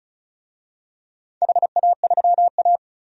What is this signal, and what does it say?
Morse code, heard as a keyed tone, about 36 wpm, 700 Hz. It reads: HA3A
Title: DXCC-100 Call Signs - 4 Characters - 35wpm